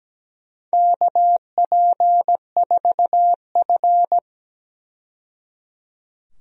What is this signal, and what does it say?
Morse code, heard as a keyed tone, about 17 wpm, 700 Hz. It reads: KP4F
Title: DXCC-100 Call Signs - 4 Characters - 17wpm